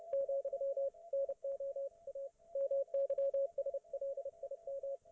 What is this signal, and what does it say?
Morse code, heard as a keyed tone, about 30 wpm, 550 Hz, with radio noise. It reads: , NO A MYSLIM